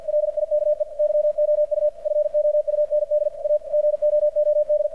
Morse code, heard as a keyed tone, about 53 wpm, 600 Hz, with radio noise. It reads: LIKE YOU FOUND A GOOD
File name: archived / CRSnet